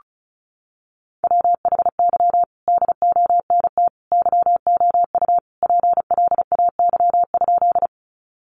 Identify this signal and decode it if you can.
Morse code, heard as a keyed tone, about 34 wpm, 700 Hz. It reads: WHY DONT YOU PLAY?
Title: Sentences from Top 300 Words 35wpm